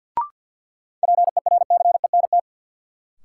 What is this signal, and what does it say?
Morse code, heard as a keyed tone, about 49 wpm, 700 Hz. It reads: PERCENT
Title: Sets of 2 Words - New Word Drill - 50wpm